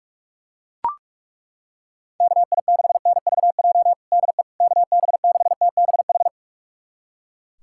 Morse code, heard as a keyed tone, about 42 wpm, 690 Hz. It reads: KI6NVJ DE KB6TBH
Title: USA Callsign DE Callsign - 6 Characters - 45wpm